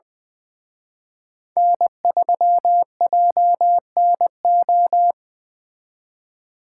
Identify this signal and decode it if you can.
Morse code, heard as a keyed tone, about 24 wpm, 700 Hz. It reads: N3JNO
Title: Speed Racing - USA Call Signs 20wpm